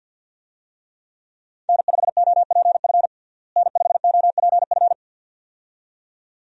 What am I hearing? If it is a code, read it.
Morse code, heard as a keyed tone, about 51 wpm, 700 Hz. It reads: N5OPF N5OPF
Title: USA Call Signs - 5 Characters - Repeated 2x - 50wpm